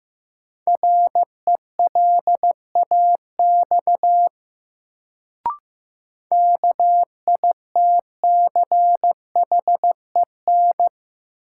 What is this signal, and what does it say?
Morse code, heard as a keyed tone, about 15 wpm, 700 Hz. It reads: RELAX KITCHEN
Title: Sentences from Top 1400 Words - New Word Drill Alt - 15wpm